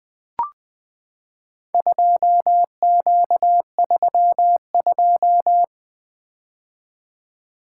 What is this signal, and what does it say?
Morse code, heard as a keyed tone, about 20 wpm, 700 Hz. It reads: 2Q32
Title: Four Letter-Number Combinations - 20wpm